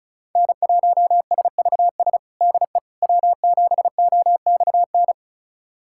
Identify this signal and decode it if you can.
Morse code, heard as a keyed tone, about 36 wpm, 700 Hz. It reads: N1SVS DE W7OXN
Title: USA Callsign DE Callsign - 5 Characters - 35wpm